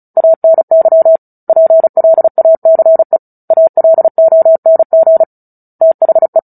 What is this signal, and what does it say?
Morse code, heard as a keyed tone, about 34 wpm, 650 Hz. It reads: ANY PLACE ALONG THE
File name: Book Two / 09-Chapter 9 - 35 WPM